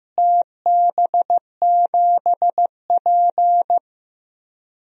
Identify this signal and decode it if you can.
Morse code, heard as a keyed tone, about 15 wpm, 700 Hz. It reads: TB7P
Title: DXCC-100 Call Signs - 4 Characters - Repeated 3x - 15wpm